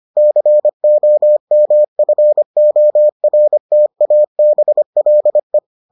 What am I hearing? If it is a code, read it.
Morse code, heard as a keyed tone, about 25 wpm, 600 Hz. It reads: COMFORTABLE